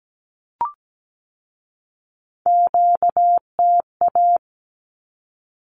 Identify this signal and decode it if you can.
Morse code, heard as a keyed tone, about 17 wpm, 700 Hz. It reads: QTA
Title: Q-Codes with Spoken Meaning - 17wpm